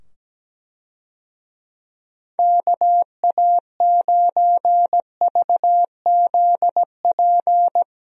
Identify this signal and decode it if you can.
Morse code, heard as a keyed tone, about 17 wpm, 700 Hz. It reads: KA9VZP
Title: USA Call Signs - 6 Characters - 17wpm